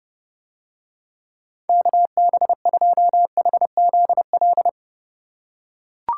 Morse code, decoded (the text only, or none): KB2HZL